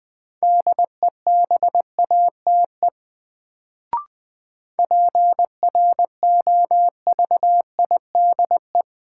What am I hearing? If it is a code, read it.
Morse code, heard as a keyed tone, about 20 wpm, 700 Hz. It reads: DEBATE PROVIDE
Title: Sentences from Top 1100 Words - New Word Drill Alt - 20wpm